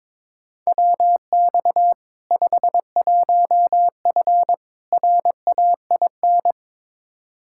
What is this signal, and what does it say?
Morse code, heard as a keyed tone, about 22 wpm, 700 Hz. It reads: WX 51F RAIN
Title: K1USN SST QSO Practice 22wpm